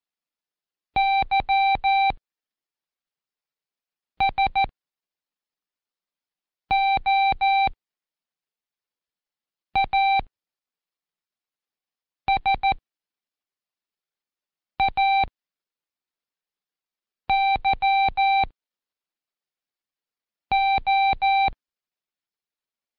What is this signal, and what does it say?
Morse code, heard as a keyed tone, about 13 wpm, 780 Hz. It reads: Y S O A S A Y O
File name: Code Quick / AudioDownld1 / Seg17-Day2-Prac YOSA